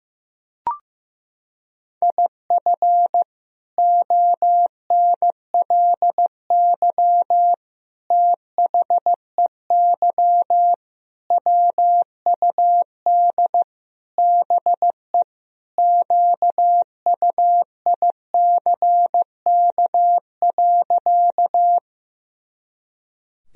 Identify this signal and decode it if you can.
Morse code, heard as a keyed tone, about 14 wpm, 700 Hz. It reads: IF ONLY THEY WUD BE QUICK.
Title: Sentences from Top 500 Words and Abbreviations 15wpm